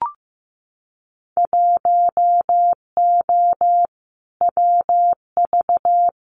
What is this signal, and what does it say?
Morse code, heard as a keyed tone, about 15 wpm, 700 Hz. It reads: 1O WV
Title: Winter Field Day Exchanges 15wpm